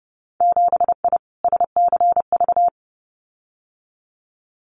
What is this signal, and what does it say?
Morse code, heard as a keyed tone, about 30 wpm, 700 Hz. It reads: 7I SCV
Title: Winter Field Day Exchanges Encore 30wpm